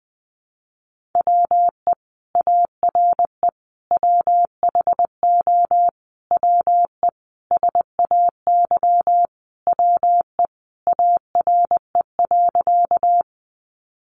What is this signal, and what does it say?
Morse code, heard as a keyed tone, about 20 wpm, 700 Hz. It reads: WE ARE WHO WE SAY WE ARE.
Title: Sentences from Top 200 Words 20wpm